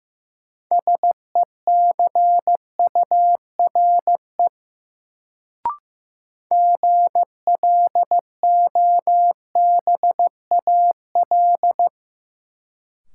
Morse code, heard as a keyed tone, about 15 wpm, 700 Hz. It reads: SECURE GLOBAL
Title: Sentences from Top 1200 Words - New Word Drill - 15wpm